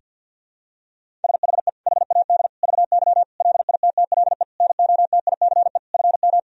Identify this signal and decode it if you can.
Morse code, heard as a keyed tone, about 51 wpm, 700 Hz. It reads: SHE HAD VY LITTLE NOTICE FM
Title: Sentences from Top 400 Words and Abbreviations 50wpm